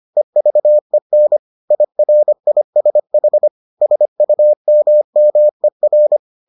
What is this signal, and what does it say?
Morse code, heard as a keyed tone, about 25 wpm, 600 Hz. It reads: EVEN IRISH SUMMER